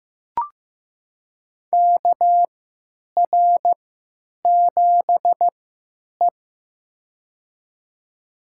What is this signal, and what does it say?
Morse code, heard as a keyed tone, about 15 wpm, spaced as for 9 wpm, 700 Hz. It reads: KR7E
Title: USA Call Signs - 4 Characters - 3x Character Spacing - 15wpm